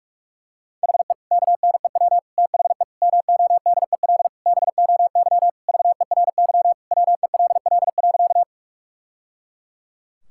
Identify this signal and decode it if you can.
Morse code, heard as a keyed tone, about 42 wpm, 700 Hz. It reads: HE KNEW THE MODEL BOY VERY WELL.
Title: Sentences from Top 1000 Words - Review Part 5 - 45wpm